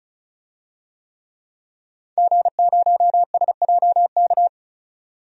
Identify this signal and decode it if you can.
Morse code, heard as a keyed tone, about 34 wpm, 700 Hz. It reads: G0SJK
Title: DXCC-100 Call Signs - 5 Characters - 35wpm